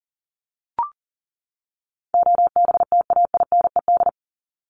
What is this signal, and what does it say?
Morse code, heard as a keyed tone, about 40 wpm, 700 Hz. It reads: OBTAINED
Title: ED Suffix - Top 100 Words with Base Words 40wpm